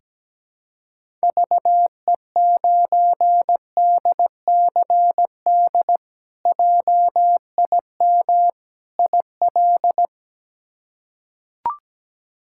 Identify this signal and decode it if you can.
Morse code, heard as a keyed tone, about 17 wpm, 700 Hz. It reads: VE9DCD JIM IL
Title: K1USN SST QSO Practice - 17wpm